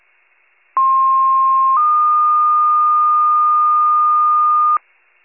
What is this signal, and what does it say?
Two-tone page, heard as a one-second tone, then a three-second tone, about 1060 then 1220 Hz, with noise. It is heard from 0.8 to 4.8 s.